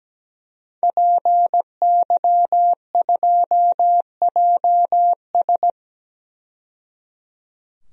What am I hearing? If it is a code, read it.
Morse code, heard as a keyed tone, about 17 wpm, 700 Hz. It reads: PY2JS
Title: DXCC-100 Call Signs - 5 Characters - 17wpm